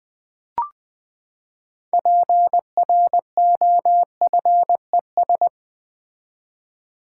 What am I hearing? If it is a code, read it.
Morse code, heard as a keyed tone, about 20 wpm, 700 Hz. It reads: PROFES
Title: Build A Word - 6 Letters At A Time - 12-Letter Words - 20wpm